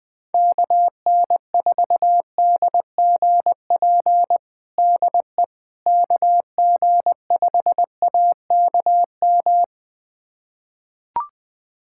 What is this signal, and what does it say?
Morse code, heard as a keyed tone, about 21 wpm, 700 Hz. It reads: KN4DGP DE KG5AKM
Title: USA Callsign DE Callsign - 6 Characters - 20wpm